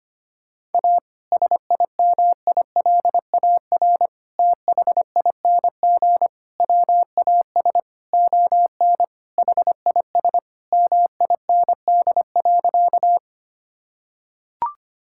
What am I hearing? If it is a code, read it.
Morse code, heard as a keyed tone, about 25 wpm, 700 Hz. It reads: A SIMILAR THING WAS ON HIS MIND.
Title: Sentences from Top 1000 Words - Review Part 5 - 25wpm